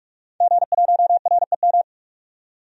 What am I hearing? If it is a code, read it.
Morse code, heard as a keyed tone, about 44 wpm, 700 Hz. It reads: G1REM
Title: DXCC-100 Call Signs - 5 Characters - 45wpm